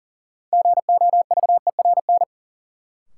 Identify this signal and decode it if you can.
Morse code, heard as a keyed tone, about 39 wpm, 700 Hz. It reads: GOVERN
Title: Sentences from Top 1000 Words - Review Part 2 - New Word Drill - 40wpm